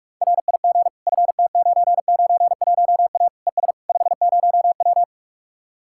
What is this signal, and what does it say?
Morse code, heard as a keyed tone, about 49 wpm, 700 Hz. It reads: RIG FT991A ES 50W